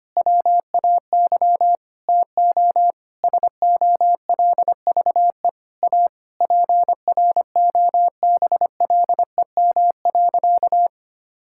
Morse code, heard as a keyed tone, about 25 wpm, 700 Hz. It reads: WAY TO SOLVE A PROBLEM.
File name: practice / 25wpm / Sentences with 9-Word Phrases - 25wpm